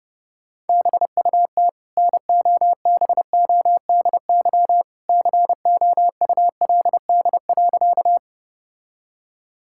Morse code, heard as a keyed tone, about 32 wpm, 700 Hz. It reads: BUT NOBODY COULD.
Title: Sentences from Top 1600 Words 30wpm